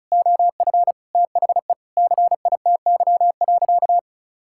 Morse code, heard as a keyed tone, about 36 wpm, 700 Hz. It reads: OF THE CITY.
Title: Sentences from Top 1000 Words - Review Part 5 - 35wpm